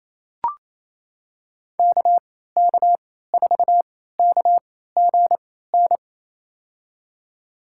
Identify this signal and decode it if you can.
Morse code, heard as a keyed tone, about 27 wpm, spaced as for 16 wpm, 700 Hz. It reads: KK4KGN
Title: USA Call Signs - 6 Characters - 3x Character Spacing - 28wpm